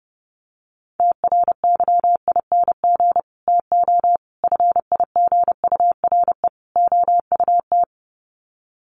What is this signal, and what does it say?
Morse code, heard as a keyed tone, about 30 wpm, 700 Hz. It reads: TRYING TO FIGURE OUT
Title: Sets of 4 Words 30wpm